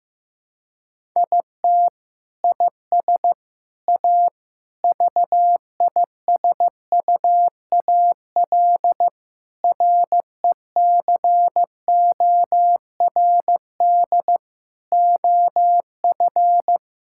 Morse code, heard as a keyed tone, about 14 wpm, 700 Hz. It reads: IT IS A VISUAL RECORD OF
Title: Sentences from Top 1000 Words - Review Part 6 - 15wpm